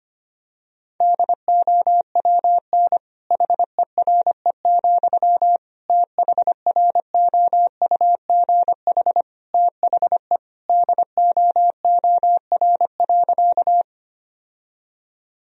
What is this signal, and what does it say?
Morse code, heard as a keyed tone, about 25 wpm, 700 Hz. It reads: DOWN HERE, THROUGH THE DOOR.
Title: Sentences from Top 1000 Words - Review Part 1 - 25wpm